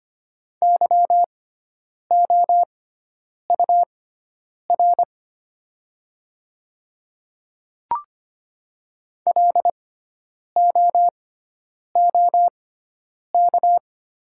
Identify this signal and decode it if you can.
Morse code, heard as a keyed tone, about 25 wpm, spaced as for 9 wpm, 690 Hz. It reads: YOUR LOOK
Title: 6x Character Spacing - Top 200 Words - 25wpm